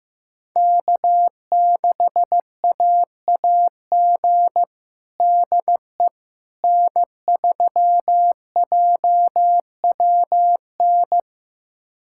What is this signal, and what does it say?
Morse code, heard as a keyed tone, about 15 wpm, 700 Hz. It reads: K6AAG DE N3JWN
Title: USA Callsign DE Callsign - 5 Characters - 15wpm